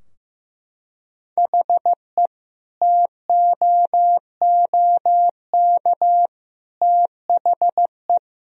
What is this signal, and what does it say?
Morse code, heard as a keyed tone, about 15 wpm, 700 Hz. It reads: HE TOOK THE
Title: Sentences from Top 1000 Words - Review Part 2 - 15wpm